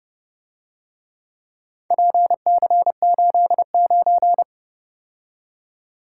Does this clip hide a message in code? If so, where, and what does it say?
Yes: Morse code from 1.9 to 4.4 s, heard as a keyed tone, about 31 wpm, 700 Hz. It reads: PC89